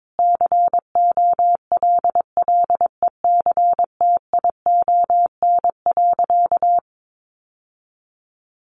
Morse code, heard as a keyed tone, about 22 wpm, 700 Hz. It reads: COLLECTION.
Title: Sentences from Top 1200 Words 22wpm